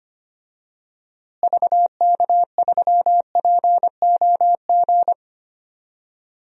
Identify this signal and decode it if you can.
Morse code, heard as a keyed tone, about 25 wpm, 700 Hz. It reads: VK3POG
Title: DXCC-100 Call Signs - 6 Characters - Repeated 2x - 25wpm